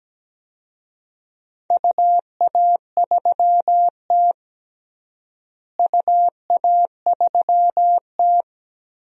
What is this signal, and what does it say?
Morse code, heard as a keyed tone, about 17 wpm, 700 Hz. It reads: UA3T UA3T
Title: DXCC-100 Call Signs - 4 Characters - Repeated 3x - 17wpm